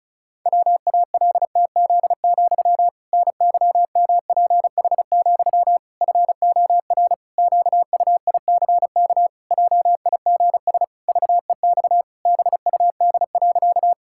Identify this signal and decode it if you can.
Morse code, heard as a keyed tone, about 36 wpm, 700 Hz. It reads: WALTZ, NYMPH, FOR QUICK JIGS VEX BUD.
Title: Pangram Challenge v2 - 35wpm